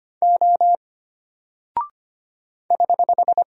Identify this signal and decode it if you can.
Morse code, heard as a keyed tone, about 25 wpm, 700 Hz. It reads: O <HH>
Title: Single Character (Letter:Number:Punctuation:Prosign) Alt 25wpm